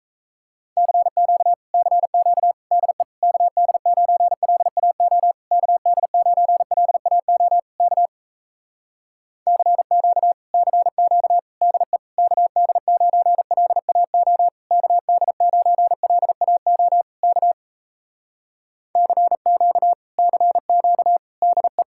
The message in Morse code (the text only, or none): CQ CQ DE KD9LAO KD9LAO K CQ CQ DE KD9LAO KD9LAO K CQ CQ DE